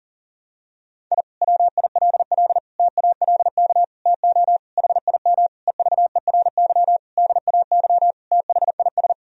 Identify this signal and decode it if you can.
Morse code, heard as a keyed tone, about 40 wpm, 700 Hz. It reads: I WILL TALK TO HIM EVERY DAY THIS
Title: Sentences from Top 1000 Words - Review Part 2 - 40wpm